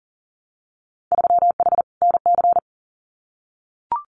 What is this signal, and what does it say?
Morse code, heard as a keyed tone, about 41 wpm, 700 Hz. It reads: 3H NC
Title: Winter Field Day Exchanges 40wpm